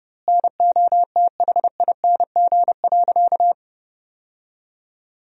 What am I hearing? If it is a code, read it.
Morse code, heard as a keyed tone, about 30 wpm, 700 Hz. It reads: NOTHING.
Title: 2x Word-Spacing - Sentences from Top 500 Words - 30wpm